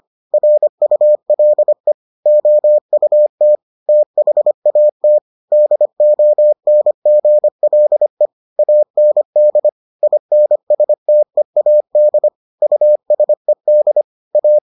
Morse code, heard as a keyed tone, about 25 wpm, 600 Hz. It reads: RULE OUT THAT DONGLE AND INSTEAD USED A